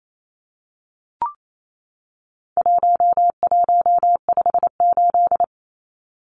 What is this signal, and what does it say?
Morse code, heard as a keyed tone, about 28 wpm, 700 Hz. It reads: 1158
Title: Numbers - 4-Digit 28wpm